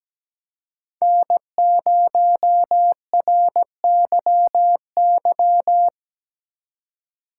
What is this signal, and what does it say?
Morse code, heard as a keyed tone, about 17 wpm, 700 Hz. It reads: N0RYY
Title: USA Call Signs - 5 Characters - 2x Character Spacing - 17wpm